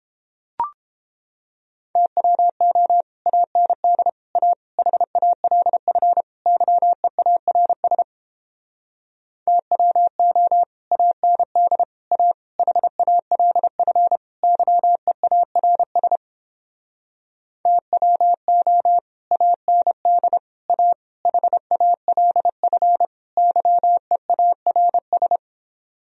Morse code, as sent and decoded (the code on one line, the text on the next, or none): - .-- --- / .- -. -.. / .- / .... .- .-.. ..-. / -.-- . .- .-. ... / - .-- --- / .- -. -.. / .- / .... .- .-.. ..-. / -.-- . .- .-. ... / - .-- --- / .- -. -.. / .- / .... .- .-.. ..-. / -.-- . .- .-. ...
TWO AND A HALF YEARS TWO AND A HALF YEARS TWO AND A HALF YEARS